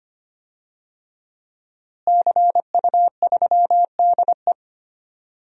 Morse code, heard as a keyed tone, about 25 wpm, 700 Hz. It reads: CU3DE
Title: DXCC-100 Call Signs - 5 Characters - Repeated 3x - 25wpm